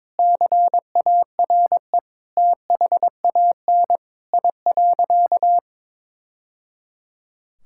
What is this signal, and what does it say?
Morse code, heard as a keyed tone, about 22 wpm, 700 Hz. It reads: CARE THAN I.
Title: Sentences from Top 1000 Words - Review Part 5 - 22wpm